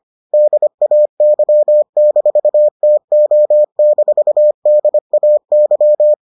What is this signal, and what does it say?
Morse code, heard as a keyed tone, about 25 wpm, 600 Hz. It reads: DAY-TO-DAY